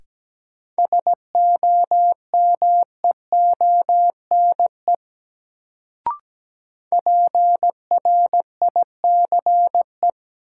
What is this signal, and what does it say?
Morse code, heard as a keyed tone, about 17 wpm, 700 Hz. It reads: SOMEONE PRICE
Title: Sentences from Top 600 Words - New Word Drill - 17wpm